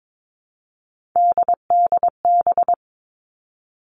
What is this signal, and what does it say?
Morse code, heard as a keyed tone, about 22 wpm, 700 Hz. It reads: DDB
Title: All Cut Numbers - 3-Digit 22wpm